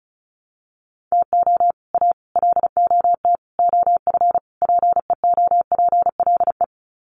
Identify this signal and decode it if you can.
Morse code, heard as a keyed tone, about 34 wpm, 700 Hz. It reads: TO A LOT OF PEOPLE
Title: Sets of 5 Words Encore 35wpm